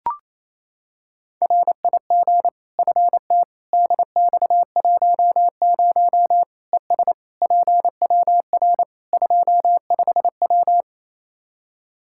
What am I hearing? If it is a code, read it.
Morse code, heard as a keyed tone, about 28 wpm, 700 Hz. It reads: RIG FT DX10 ES PWR 25W
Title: Practice QSO - Exchange 2 - Info - 28wpm